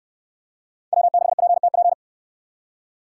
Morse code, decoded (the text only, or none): W6PTP